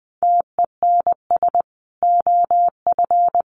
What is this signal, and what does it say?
Morse code, heard as a keyed tone, about 20 wpm, 700 Hz. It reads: TENS OF
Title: Sets of 5 Words Encore2 20wpm